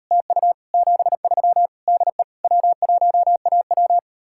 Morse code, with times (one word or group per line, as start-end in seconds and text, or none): 0.1-0.5 TU
0.7-1.7 73
1.9-2.2 DE
2.4-4.0 W1AW